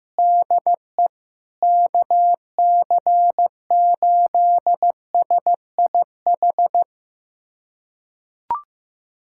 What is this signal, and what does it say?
Morse code, heard as a keyed tone, about 15 wpm, 700 Hz. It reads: DE KC8SIH
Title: USA Callsign DE Callsign 15wpm